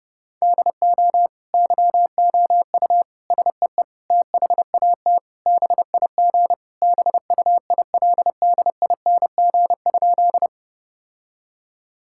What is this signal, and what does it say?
Morse code, heard as a keyed tone, about 30 wpm, 700 Hz. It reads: DO YOU SEE THAT BIG BUILDING?
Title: Sentences from Top 1000 Words - Review Part 5 - 30wpm